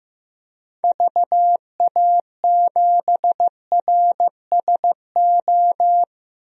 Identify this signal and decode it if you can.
Morse code, heard as a keyed tone, about 15 wpm, 700 Hz. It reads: VA7RSO
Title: DXCC-100 Call Signs - 6 Characters - 15wpm